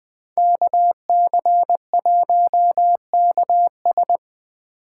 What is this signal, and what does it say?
Morse code, heard as a keyed tone, about 20 wpm, 700 Hz. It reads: KC1KS